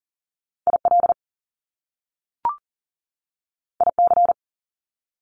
Morse code, heard as a keyed tone, about 42 wpm, 700 Hz. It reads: IL IC
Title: Top Two Letter Combinations in Words 40wpm